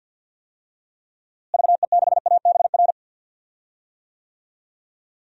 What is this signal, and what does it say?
Morse code, heard as a keyed tone, about 52 wpm, 700 Hz. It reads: VE6ABR
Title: DXCC-100 Call Signs - 6 Characters - Repeated 2x - 50wpm